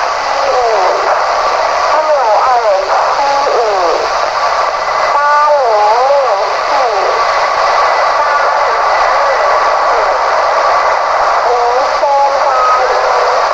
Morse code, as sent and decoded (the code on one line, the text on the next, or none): none